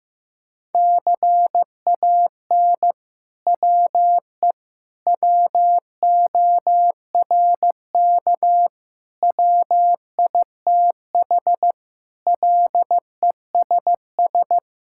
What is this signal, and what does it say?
Morse code, heard as a keyed tone, about 16 wpm, 700 Hz. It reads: CAN WE WORK WITH LESS
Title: Sentences from Top 1000 Words - Review Part 5 - 15wpm